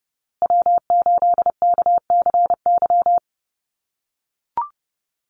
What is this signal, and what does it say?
Morse code, heard as a keyed tone, about 30 wpm, 700 Hz. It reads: W8KCY
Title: USA Call Signs Repeated 3x 30wpm